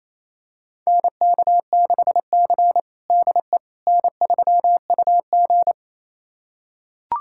Morse code, heard as a keyed tone, about 28 wpm, 700 Hz. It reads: NK6C DE N3UG
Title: USA Callsign DE Callsign - 4 Characters - 28wpm